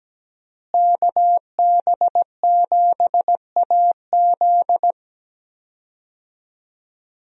Morse code, heard as a keyed tone, about 17 wpm, 700 Hz. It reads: KB7AZ
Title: USA Call Signs - 5 Characters - Repeated 3x - 17wpm